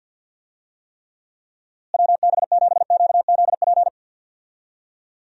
Morse code, heard as a keyed tone, about 50 wpm, 700 Hz. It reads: WB7QZP
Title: USA Call Signs - 6 Characters - Repeated 2x - 50wpm